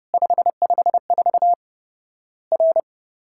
Morse code, heard as a keyed tone, about 30 wpm, 700 Hz. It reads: 554 R
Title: Worked All Europe DX Contest - QTC Exchange - 30wpm